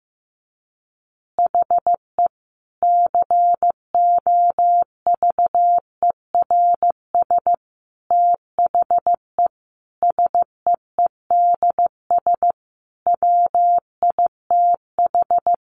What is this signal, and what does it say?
Morse code, heard as a keyed tone, about 15 wpm, 700 Hz. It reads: HE COVERS THE SEEDS WITH
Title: Easy Phrases 15wpm